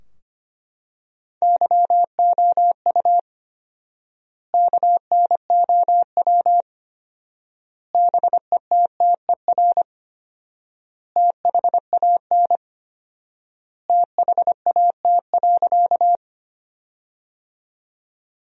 Morse code, heard as a keyed tone, about 25 wpm, 700 Hz. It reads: YOU KNOW BETTER THAN THAT.
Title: 4x Word-Spacing - Sentences from Top 400 Words - 25wpm